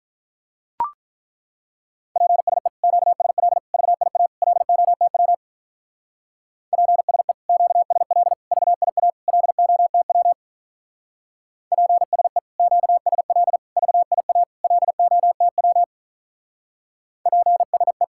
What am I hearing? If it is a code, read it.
Morse code, heard as a keyed tone, about 43 wpm, 700 Hz. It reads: PSE QSL VIA LOTW PSE QSL VIA LOTW PSE QSL VIA LOTW PSE
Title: Speed-Racing Common QSO Phrases - 35wpm